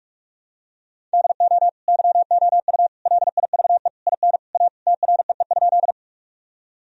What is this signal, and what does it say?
Morse code, heard as a keyed tone, about 42 wpm, 700 Hz. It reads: DO YOU LIVE IN A TREE?
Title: Sentences from Top 1000 Words - Review Part 1 - 45wpm